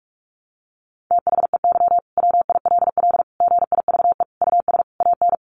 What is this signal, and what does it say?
Morse code, heard as a keyed tone, about 44 wpm, 700 Hz. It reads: THEY WILL GIVE US AN
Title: Sentences from Top 600 Words 45wpm